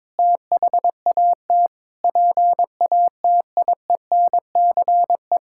Morse code, heard as a keyed tone, about 22 wpm, 700 Hz. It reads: THAT PATIENCE
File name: Sentences with 7-Word Phrases - 22wpm